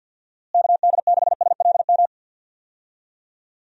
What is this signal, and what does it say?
Morse code, heard as a keyed tone, about 52 wpm, 700 Hz. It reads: KD6SLM
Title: USA Call Signs - 6 Characters - Repeated 2x - 50wpm